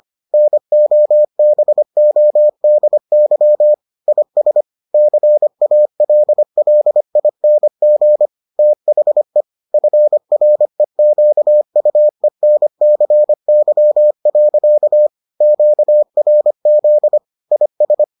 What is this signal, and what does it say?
Morse code, heard as a keyed tone, about 25 wpm, 600 Hz. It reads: NOBODY IS CALLING THE FREQUENCY. QRZ IS